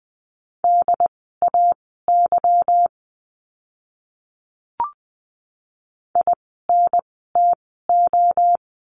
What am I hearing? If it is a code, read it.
Morse code, heard as a keyed tone, about 20 wpm, spaced as for 14 wpm, 700 Hz. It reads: DAY INTO
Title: 206 Lesson 33f - Letter j - Words 20wpm